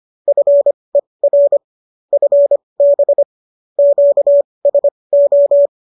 Morse code, heard as a keyed tone, about 25 wpm, spaced as for 20 wpm, 570 Hz. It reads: FER FB QSO